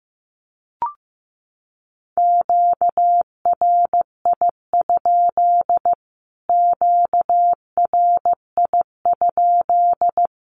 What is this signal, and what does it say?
Morse code, heard as a keyed tone, about 15 wpm, 700 Hz. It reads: QRI? QRI?
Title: Q-Codes Repeated 3x 15wpm